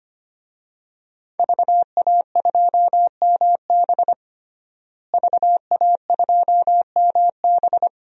Morse code, heard as a keyed tone, about 25 wpm, 700 Hz. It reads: VA2MB VA2MB
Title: DXCC-100 Call Signs - 5 Characters - Repeated 3x - 25wpm